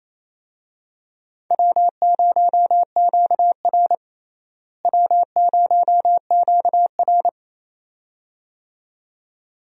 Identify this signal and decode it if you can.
Morse code, heard as a keyed tone, about 28 wpm, 700 Hz. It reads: W0QR W0QR
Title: USA Call Signs - 4 Characters - Repeated 2x - 28wpm